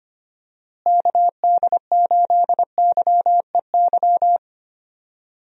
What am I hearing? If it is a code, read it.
Morse code, heard as a keyed tone, about 25 wpm, 700 Hz. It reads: KD8YEY
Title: USA Call Signs - 6 Characters - Repeated 2x - 25wpm